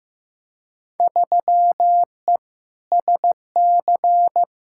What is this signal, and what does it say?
Morse code, heard as a keyed tone, about 15 wpm, 700 Hz. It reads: 3E SC
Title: Field Day Exchanges Encore - 15wpm